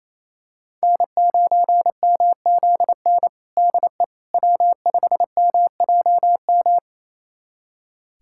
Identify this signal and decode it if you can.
Morse code, heard as a keyed tone, about 28 wpm, 700 Hz. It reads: N9MZN DE W5MJM
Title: USA Callsign DE Callsign - 5 Characters - 28wpm